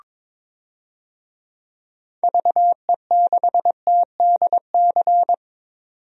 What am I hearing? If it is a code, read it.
Morse code, heard as a keyed tone, about 22 wpm, 700 Hz. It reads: VE6TDC
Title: DXCC-100 Call Signs - 6 Characters - 22wpm